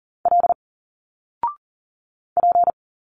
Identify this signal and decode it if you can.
Morse code, heard as a keyed tone, about 40 wpm, 700 Hz. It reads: L P
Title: Single Letters 40wpm